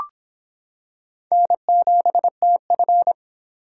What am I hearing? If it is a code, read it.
Morse code, heard as a keyed tone, about 25 wpm, 700 Hz. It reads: N7TF